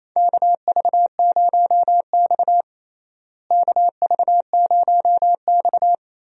Morse code, heard as a keyed tone, about 28 wpm, 700 Hz. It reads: KV0X KV0X
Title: USA Call Signs - 4 Characters - Repeated 2x - 28wpm